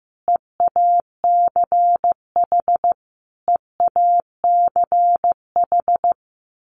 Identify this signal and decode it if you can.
Morse code, heard as a keyed tone, about 15 wpm, 700 Hz. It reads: EACH EACH
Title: Top 500 Words Repeated 15wpm